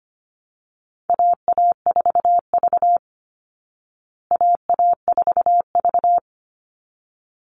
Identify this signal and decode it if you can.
Morse code, heard as a keyed tone, about 25 wpm, 700 Hz. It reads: AA4V AA4V
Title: Call Signs Repeated 3x - 25wpm